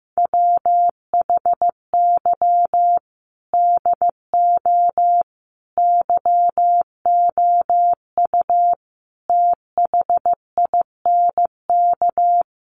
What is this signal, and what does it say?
Morse code, heard as a keyed tone, about 15 wpm, 700 Hz. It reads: WHY DO YOU THINK
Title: Sets of 5 Words Encore2 15wpm